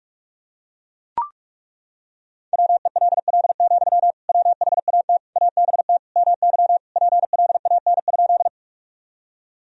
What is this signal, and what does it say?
Morse code, heard as a keyed tone, about 41 wpm, 700 Hz. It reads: WELL, WHAT ABT MY PLAN?
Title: Sentences from Top 500 Words and Abbreviations 45wpm